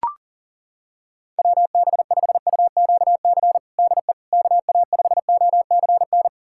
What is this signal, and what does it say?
Morse code, heard as a keyed tone, about 40 wpm, 700 Hz. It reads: WB5UQC DE KA5OCN
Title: USA Callsign DE Callsign - 6 Characters - 40wpm